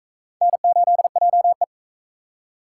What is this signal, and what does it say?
Morse code, heard as a keyed tone, about 41 wpm, 700 Hz. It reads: N8JE